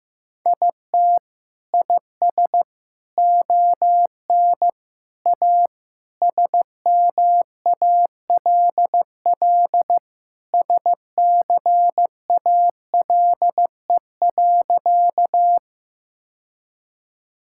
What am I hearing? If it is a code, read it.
Morse code, heard as a keyed tone, about 15 wpm, 700 Hz. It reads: IT IS ON A SMALL SCALE.
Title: Sentences from Top 1000 Words - Review Part 5 - 15wpm